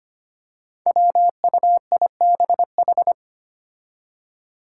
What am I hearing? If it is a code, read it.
Morse code, heard as a keyed tone, about 25 wpm, 700 Hz. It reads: WUIBH
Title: Five Challenging Letter-Number Combinations 25wpm